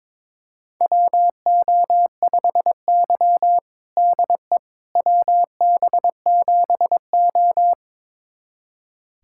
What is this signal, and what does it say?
Morse code, heard as a keyed tone, about 22 wpm, 700 Hz. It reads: WO5Y DE WB7O
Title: USA Callsign DE Callsign - 4 Characters - 22wpm